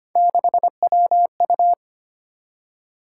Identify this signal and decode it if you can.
Morse code, heard as a keyed tone, about 25 wpm, 700 Hz. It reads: 6WU
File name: Three Letter-Number Combinations - 25wpm